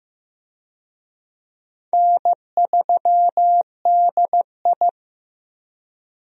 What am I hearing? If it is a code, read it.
Morse code, heard as a keyed tone, about 15 wpm, 700 Hz. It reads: N3DI
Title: USA Call Signs - 4 Characters - Repeated 3x - 15wpm